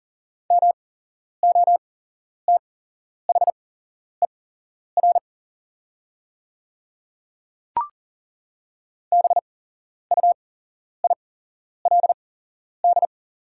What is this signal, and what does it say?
Morse code, heard as a keyed tone, about 40 wpm, spaced as for 11 wpm, 700 Hz. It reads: MOTHER BUILD
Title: 8x Character Spacing - Top 200 Words - 40wpm